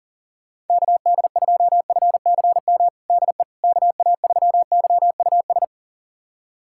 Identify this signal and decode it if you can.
Morse code, heard as a keyed tone, about 37 wpm, 700 Hz. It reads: KD2FCM DE KA3YUS
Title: USA Callsign DE Callsign - 6 Characters - 40wpm